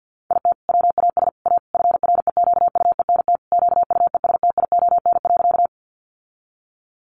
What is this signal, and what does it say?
Morse code, heard as a keyed tone, about 50 wpm, 700 Hz. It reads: IT WAS A FREQUENT QUESTION.
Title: Sentences from Top 1400 Words 50wpm